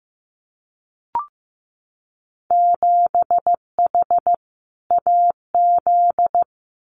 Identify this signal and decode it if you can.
Morse code, heard as a keyed tone, about 15 wpm, 700 Hz. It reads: 7H AZ
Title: Winter Field Day Exchanges Encore 15wpm